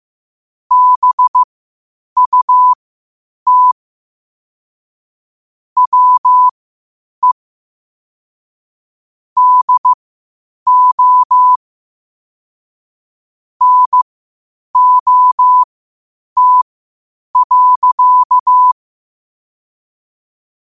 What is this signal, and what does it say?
Morse code, heard as a keyed tone, about 15 wpm, spaced as for 8 wpm, 1000 Hz. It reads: BUT WE DO NOT.
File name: qotdfiles / 8wpm